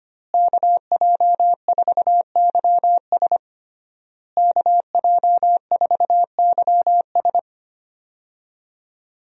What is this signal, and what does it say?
Morse code, heard as a keyed tone, about 25 wpm, 700 Hz. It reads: KJ4YS KJ4YS
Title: USA Call Signs - 5 Characters - Repeated 3x - 25wpm